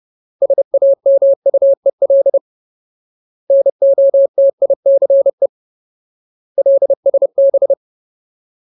Morse code, heard as a keyed tone, about 30 wpm, 550 Hz. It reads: SAMUEL NOTICE LSB
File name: Ultimate Word List / 30WPM Ultimate